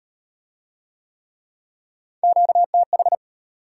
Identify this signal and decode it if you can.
Morse code, heard as a keyed tone, about 34 wpm, 700 Hz. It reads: QTH